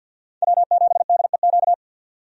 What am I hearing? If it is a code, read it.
Morse code, heard as a keyed tone, about 47 wpm, 700 Hz. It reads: W7DEQ
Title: USA Call Signs - 5 Characters - 50wpm